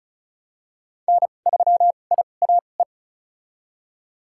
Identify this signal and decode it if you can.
Morse code, heard as a keyed tone, about 36 wpm, spaced as for 25 wpm, 700 Hz. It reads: N3IAE
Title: USA Call Signs - 2x Character Spacing - 35wpm